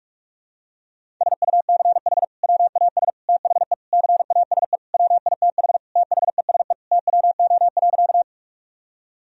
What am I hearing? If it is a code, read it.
Morse code, heard as a keyed tone, about 46 wpm, 700 Hz. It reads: SUCH WAS THE CASE WITH THESE TWO.
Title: Sentences from Top 1000 Words - Review Part 1 - 45wpm